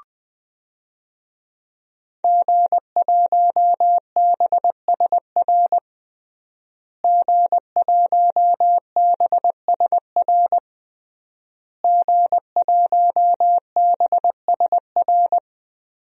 Morse code, heard as a keyed tone, about 20 wpm, 700 Hz. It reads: G1BSR G1BSR G1BSR
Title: DXCC-100 Call Signs - 5 Characters - Repeated 3x - 20wpm